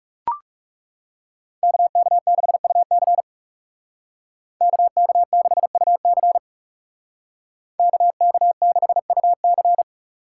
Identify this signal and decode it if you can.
Morse code, heard as a keyed tone, about 40 wpm, 700 Hz. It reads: KK6UC KK6UC KK6UC